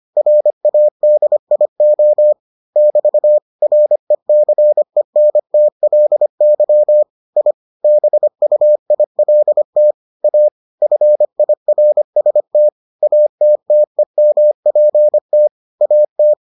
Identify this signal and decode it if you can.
Morse code, heard as a keyed tone, about 25 wpm, 600 Hz. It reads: RADIO <BT> RECENTLY I BUILT A FIRST ATTEMPT AT